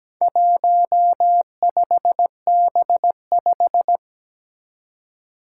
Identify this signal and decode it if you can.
Morse code, heard as a keyed tone, about 17 wpm, 700 Hz. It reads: 15B5
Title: Four Challenging Letter-Number Combinations 17wpm